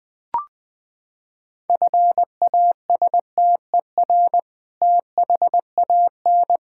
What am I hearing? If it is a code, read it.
Morse code, heard as a keyed tone, about 20 wpm, 700 Hz. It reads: FASTER THAN
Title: Sentences with 2-Word Phrases - 20wpm